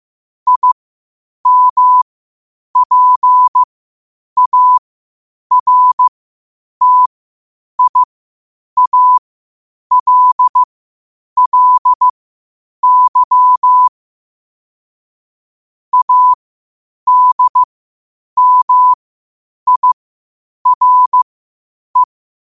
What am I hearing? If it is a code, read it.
Morse code, heard as a keyed tone, about 15 wpm, spaced as for 8 wpm, 1000 Hz. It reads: IMPARTIALLY ADMIRE
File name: qotdfiles / 8wpm